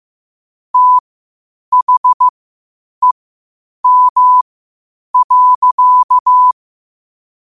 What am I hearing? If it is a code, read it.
Morse code, heard as a keyed tone, about 15 wpm, spaced as for 8 wpm, 1000 Hz. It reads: THEM.
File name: qotdfiles / 8wpm